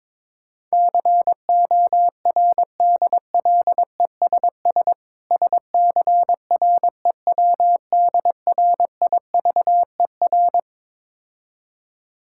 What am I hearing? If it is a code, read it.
Morse code, heard as a keyed tone, about 22 wpm, 690 Hz. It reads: CORDLESS SCREWDRIVER
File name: Difficult Words with Hints - 22wpm